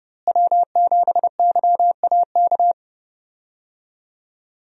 Morse code, heard as a keyed tone, about 32 wpm, 700 Hz. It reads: W7YAK
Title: USA Call Signs - 5 Characters - Repeated 3x - 30wpm